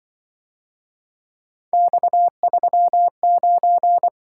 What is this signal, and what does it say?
Morse code, heard as a keyed tone, about 23 wpm, 700 Hz. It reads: X39
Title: Speed-Racing - Three Letter-Number Combinations - 20wpm